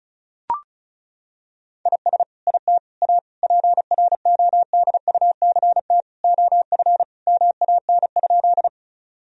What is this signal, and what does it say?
Morse code, heard as a keyed tone, about 36 wpm, 700 Hz. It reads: IS IT A PRODUCT OF MAN?
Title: Sentences from Top 1000 Words - Review Part 2 - 35wpm